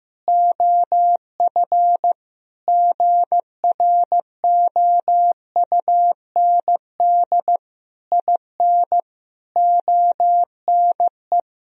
Morse code, heard as a keyed tone, about 15 wpm, 700 Hz. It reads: OF GROUND IN ONE
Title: Sentences from Top 1000 Words - Review Part 3 - 15wpm